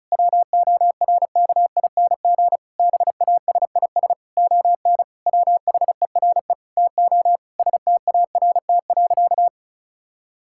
Morse code, heard as a keyed tone, about 36 wpm, 700 Hz. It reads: WORKING BASIS ON WHERE TO START.
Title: Sentences from Top 1000 Words - Review Part 5 - 35wpm